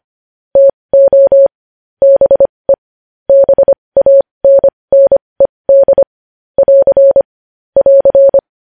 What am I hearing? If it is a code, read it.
Morse code, heard as a keyed tone, about 25 wpm, spaced as for 20 wpm, 550 Hz. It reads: TO BE BANNED <AR> <AR>
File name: podcast / News headlines-1x-25x20